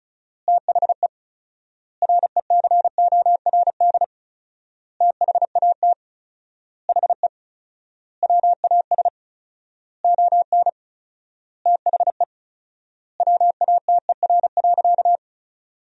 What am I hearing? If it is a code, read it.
Morse code, heard as a keyed tone, about 36 wpm, 700 Hz. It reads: THE RECORD THAT HE WAS ON THE WATER.
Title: 4x Word-Spacing - Sentences from Top 500 Words - 35wpm